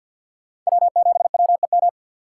Morse code, heard as a keyed tone, about 66 wpm, spaced as for 58 wpm, 700 Hz. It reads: W7WEM
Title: Speed Racing - USA 1x3 Call Signs 50wpm